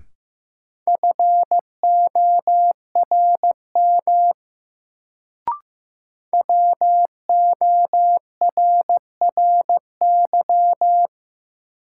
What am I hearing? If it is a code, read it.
Morse code, heard as a keyed tone, about 16 wpm, 700 Hz. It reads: FORM WORRY
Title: Sentences from Top 1100 Words - New Word Drill Alt - 15wpm